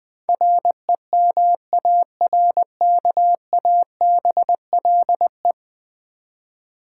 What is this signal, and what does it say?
Morse code, heard as a keyed tone, about 20 wpm, 700 Hz. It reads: REMARKABLE
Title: Ten Letter Words - 20wpm